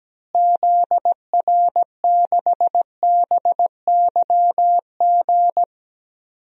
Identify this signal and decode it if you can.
Morse code, heard as a keyed tone, about 17 wpm, 700 Hz. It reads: ZR6BYG
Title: DXCC-100 Call Signs - 6 Characters - Repeated 2x - 17wpm